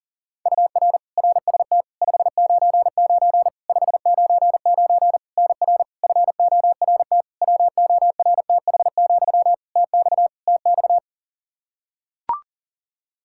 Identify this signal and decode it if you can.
Morse code, heard as a keyed tone, about 40 wpm, 700 Hz. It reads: UR RST 599 599 NR FORT WORTH, TX TX
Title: Practice QSO - Exchange 1 - Beginning - 40wpm